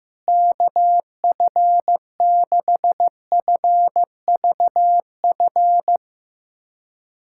Morse code, heard as a keyed tone, about 15 wpm, 700 Hz. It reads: KF6FVF
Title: USA Callsign DE Callsign - 6 Characters - 15wpm